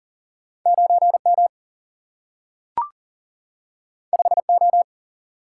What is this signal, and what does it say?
Morse code, heard as a keyed tone, about 40 wpm, 700 Hz. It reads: 9M 5O
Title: Single-Letter Single-Number 40wpm